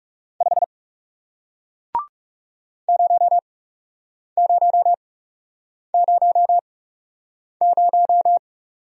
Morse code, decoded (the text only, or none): 5 0000